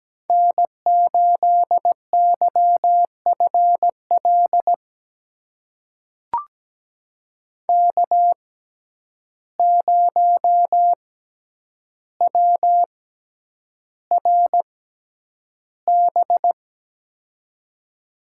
Morse code, heard as a keyed tone, about 17 wpm, spaced as for 9 wpm, 700 Hz. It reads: N8YFL K0WRB
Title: USA Call Signs - 5 Characters - 6x Character Spacing - 17wpm